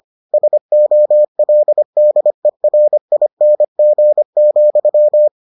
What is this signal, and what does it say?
Morse code, heard as a keyed tone, about 25 wpm, 600 Hz. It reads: SOLDERING,